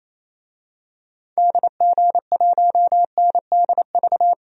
Morse code, heard as a keyed tone, about 28 wpm, 700 Hz. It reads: DG1NDV
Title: DXCC-100 Call Signs - 6 Characters - Repeated 3x - 28wpm